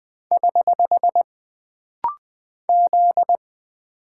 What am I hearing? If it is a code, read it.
Morse code, heard as a keyed tone, about 20 wpm, 700 Hz. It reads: <HH> Z
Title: Single Character (Letter:Number:Punctuation:Prosign) - Rapid-Fire - 20wpm